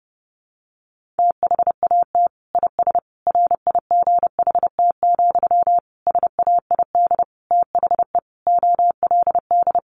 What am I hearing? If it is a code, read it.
Morse code, heard as a keyed tone, about 30 wpm, 700 Hz. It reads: THAT IS RIGHT, SAID THE OLD
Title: Sentences from Top 200 Words 30wpm